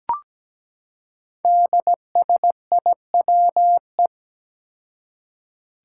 Morse code, heard as a keyed tone, about 17 wpm, 700 Hz. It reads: DSIWE
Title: Five Challenging Letter-Number Combinations 17wpm